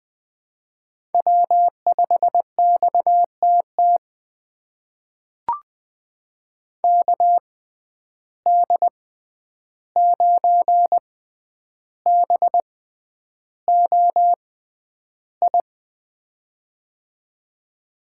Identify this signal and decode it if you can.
Morse code, heard as a keyed tone, about 20 wpm, spaced as for 10 wpm, 700 Hz. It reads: W5XTT KD9BOI
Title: USA Call Signs - 6x Character Spacing - 20wpm